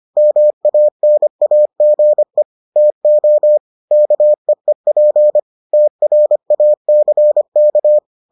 Morse code, heard as a keyed tone, about 25 wpm, 600 Hz. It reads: MANAGE TO KEEP TRACK